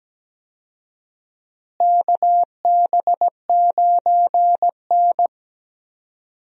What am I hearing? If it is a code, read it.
Morse code, heard as a keyed tone, about 17 wpm, 700 Hz. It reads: KB9N
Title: USA Call Signs - 4 Characters - Repeated 2x - 17wpm